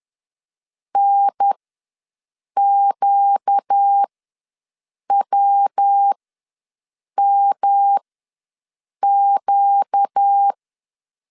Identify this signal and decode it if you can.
Morse code, heard as a keyed tone, about 10 wpm, spaced as for 6 wpm, 800 Hz. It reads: NQWMQ
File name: LEZIONI / 5 LBVC / 5 LB ws 10wpm 3616